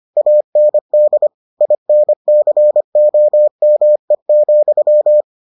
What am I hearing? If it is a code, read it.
Morse code, heard as a keyed tone, about 25 wpm, 600 Hz. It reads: AND INCOME,